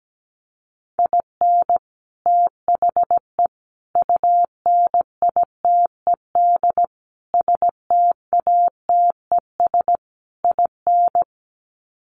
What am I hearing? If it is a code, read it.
Morse code, heard as a keyed tone, about 17 wpm, 700 Hz. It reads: IN THE UNITED STATES IN
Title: Sets of 5 Words 17wpm